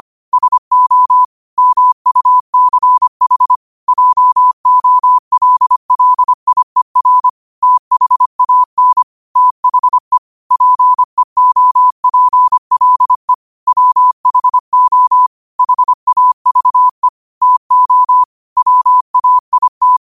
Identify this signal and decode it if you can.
Morse code, heard as a keyed tone, about 25 wpm, 1000 Hz. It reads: SO MUCH JOLLIER THAN THE PEOPLE WHO HAVE TO WAIT